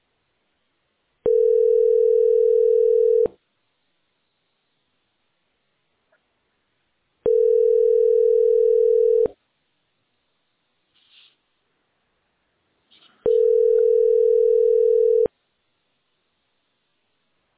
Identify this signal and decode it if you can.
Ringback tone (North America), 440 + 480 Hz.